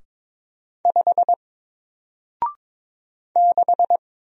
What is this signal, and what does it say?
Morse code, heard as a keyed tone, about 22 wpm, 700 Hz. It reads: 5 6
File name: Single-Letter-Number-22wpm